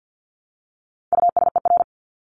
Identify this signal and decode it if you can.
Morse code, heard as a keyed tone, about 50 wpm, 700 Hz. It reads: USER
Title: Top 751-1000 Words 50wpm